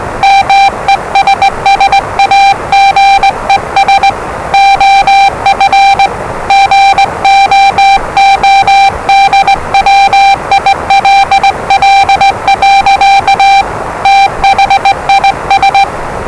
Morse code, heard as a keyed tone, about 20 wpm, 790 Hz, with noise. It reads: MESSAGES OF GOODWILL. THIS
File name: WellingtonRadioZlwCloseDownOnHfAndMf / 01-Zlw 4277khz 417.5 qrt until 0925z